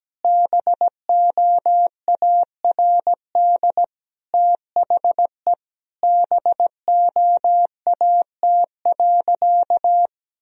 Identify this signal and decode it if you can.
Morse code, heard as a keyed tone, about 17 wpm, 700 Hz. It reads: BOARD THE BOAT.
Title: Sentences from Top 1000 Words - Review Part 4 - 17wpm